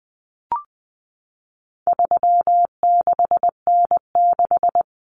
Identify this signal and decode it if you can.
Morse code, heard as a keyed tone, about 20 wpm, 700 Hz. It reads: 36N6
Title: Common Cut Numbers - 4-Digit 20wpm